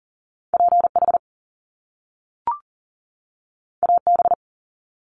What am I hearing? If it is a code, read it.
Morse code, heard as a keyed tone, about 40 wpm, 700 Hz. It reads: PH AB
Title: Top Two Letter Combinations in Words 40wpm